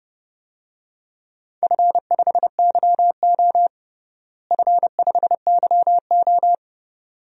Morse code, heard as a keyed tone, about 30 wpm, 700 Hz. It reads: F5YO F5YO
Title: DXCC-100 Call Signs - 4 Characters - Repeated 2x - 30wpm